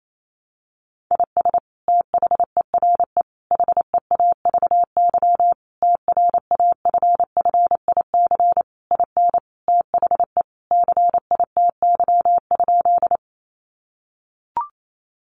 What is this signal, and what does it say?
Morse code, heard as a keyed tone, about 28 wpm, 700 Hz. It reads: IS THERE HEAVY TRAFFIC IN THE CITY?
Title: Sentences from Top 1300 Words 28wpm